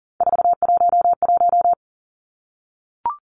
Morse code, heard as a keyed tone, about 40 wpm, 700 Hz. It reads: 411
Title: Numbers - 3-Digit 40wpm